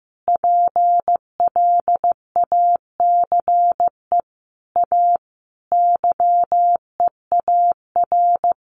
Morse code, heard as a keyed tone, about 15 wpm, 700 Hz. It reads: PLACE A YEAR
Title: Sentences from Top 1500 Words 15wpm